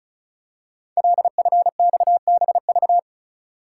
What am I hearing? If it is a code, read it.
Morse code, heard as a keyed tone, about 36 wpm, 690 Hz. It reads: LFXBV
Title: Code Groups - 5 Letters - 35wpm